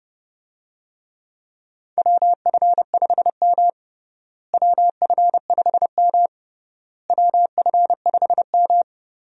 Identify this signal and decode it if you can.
Morse code, heard as a keyed tone, about 30 wpm, 700 Hz. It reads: WF5M WF5M WF5M
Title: USA Call Signs - 4 Characters - Repeated 3x - 30wpm